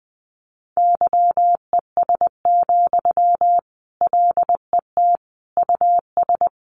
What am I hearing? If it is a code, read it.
Morse code, heard as a keyed tone, about 20 wpm, 700 Hz. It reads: YES, LET US
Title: Sentences from Top 1200 Words 20wpm